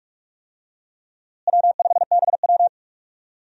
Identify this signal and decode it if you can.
Morse code, heard as a keyed tone, about 46 wpm, 700 Hz. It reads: W5BW
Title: USA Call Signs - 4 Characters - Repeated 2x - 45wpm